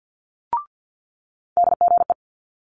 Morse code, heard as a keyed tone, about 50 wpm, 700 Hz. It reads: DGE
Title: Multi-letter Phonemes Quick Fire 50wpm